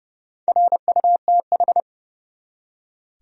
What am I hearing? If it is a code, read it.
Morse code, heard as a keyed tone, about 32 wpm, 700 Hz. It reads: RUTH
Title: USA Common Names - Max 4 Letters - 30wpm